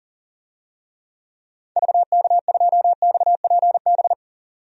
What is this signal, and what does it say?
Morse code, heard as a keyed tone, about 38 wpm, 700 Hz. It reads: VK2XPB
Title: DXCC-100 Call Signs - 6 Characters - 40wpm